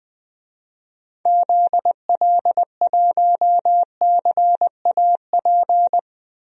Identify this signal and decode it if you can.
Morse code, heard as a keyed tone, about 20 wpm, 700 Hz. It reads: ZL1CAP
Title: DXCC-100 Call Signs - 6 Characters - Repeated 2x - 20wpm